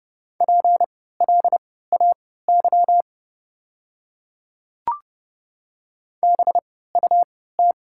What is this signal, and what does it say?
Morse code, heard as a keyed tone, about 29 wpm, spaced as for 17 wpm, 700 Hz. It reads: PLAY BUT
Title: 3x Character Spacing - Top 500 Words - 30wpm